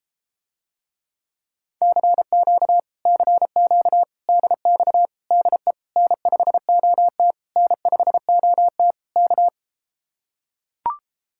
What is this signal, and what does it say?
Morse code, heard as a keyed tone, about 33 wpm, 700 Hz. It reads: CQ CQ DX DE N5OT N5OT K